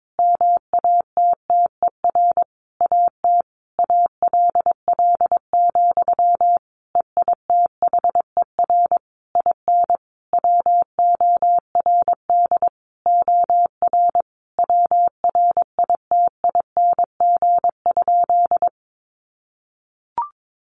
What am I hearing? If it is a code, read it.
Morse code, heard as a keyed tone, about 22 wpm, 700 Hz. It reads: MATTER AT ALL, EITHER IN WORD OR WRITING?
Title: Sentences from Top 1200 Words 22wpm